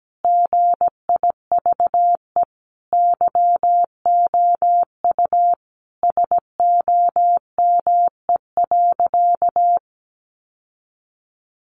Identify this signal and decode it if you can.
Morse code, heard as a keyed tone, about 17 wpm, 700 Hz. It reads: GIVE YOU SOME.
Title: Sentences from Top 800 Words 17wpm